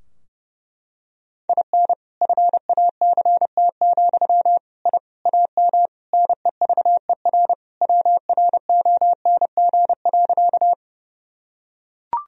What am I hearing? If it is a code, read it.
Morse code, heard as a keyed tone, about 30 wpm, 700 Hz. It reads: IN FACT, I AM NEVER WRONG.
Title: Sentences from Top 1000 Words - Review Part 3 - 30wpm